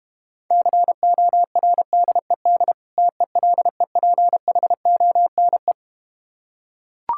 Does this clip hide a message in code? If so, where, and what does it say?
Yes: Morse code from 0.5 to 5.7 s, heard as a keyed tone, about 32 wpm, 700 Hz. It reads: CORDED TELEPHONE